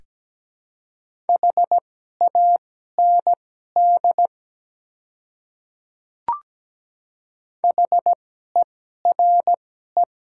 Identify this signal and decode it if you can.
Morse code, heard as a keyed tone, about 17 wpm, spaced as for 12 wpm, 700 Hz. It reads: HAND HERE
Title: 2x Character Spacing - Top 300 Words - 17wpm